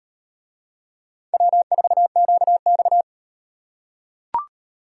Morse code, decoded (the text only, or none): W4QX